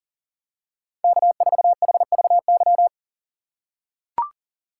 Morse code, heard as a keyed tone, about 40 wpm, 700 Hz. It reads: K4HVY
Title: USA Call Signs - 6x Character Spacing - 40wpm